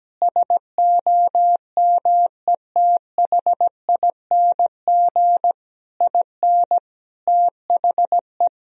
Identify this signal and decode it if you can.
Morse code, heard as a keyed tone, about 17 wpm, 700 Hz. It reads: SOMETHING IN THE
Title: Sentences from Top 1000 Words - Review Part 4 - 17wpm